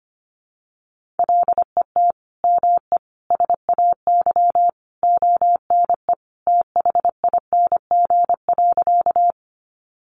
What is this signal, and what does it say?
Morse code, heard as a keyed tone, about 25 wpm, 700 Hz. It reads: LET ME SAY ONE THING.
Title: Sentences from Top 300 Words 25wpm